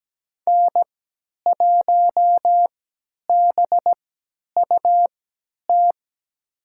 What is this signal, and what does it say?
Morse code, heard as a keyed tone, about 17 wpm, spaced as for 10 wpm, 700 Hz. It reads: N1BUT
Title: USA Call Signs - 5 Characters - 3x Character Spacing - 17wpm